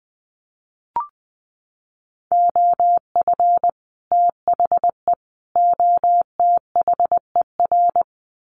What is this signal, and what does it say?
Morse code, heard as a keyed tone, about 20 wpm, 700 Hz. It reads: OF THE OTHER
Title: Sets of 3 Words 20wpm